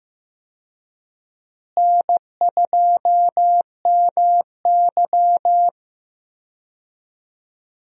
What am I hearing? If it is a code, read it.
Morse code, heard as a keyed tone, about 15 wpm, 700 Hz. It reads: N2MY
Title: USA Call Signs - 4 Characters - Repeated 2x - 15wpm